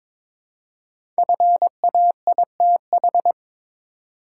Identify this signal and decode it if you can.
Morse code, heard as a keyed tone, about 22 wpm, 700 Hz. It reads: FAITH
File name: 5 Letter Words - First 3 Letters Given - 22wpm